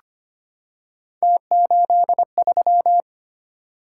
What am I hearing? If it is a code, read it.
Morse code, heard as a keyed tone, about 25 wpm, 700 Hz. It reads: T83
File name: Three Letter-Number Combinations - 25wpm